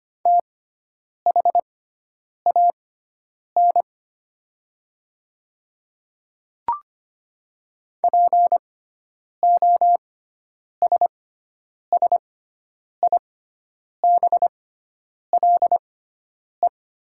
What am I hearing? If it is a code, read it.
Morse code, heard as a keyed tone, about 25 wpm, spaced as for 9 wpm, 700 Hz. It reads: THAN POSSIBLE
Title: 6x Character Spacing - Top 500 Words - 25wpm